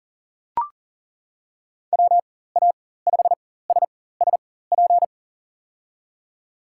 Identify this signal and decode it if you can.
Morse code, heard as a keyed tone, about 40 wpm, spaced as for 19 wpm, 700 Hz. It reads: WA5SSP
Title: USA Call Signs - 6 Characters - 4x Character Spacing - 40wpm